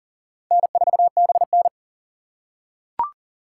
Morse code, heard as a keyed tone, about 40 wpm, 700 Hz. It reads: N4BN
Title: USA Call Signs - 4 Characters - 4x Character Spacing - 40wpm